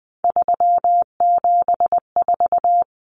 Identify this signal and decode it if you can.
Morse code, heard as a keyed tone, about 20 wpm, 700 Hz. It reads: 374
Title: Numbers - 3-Digit 20wpm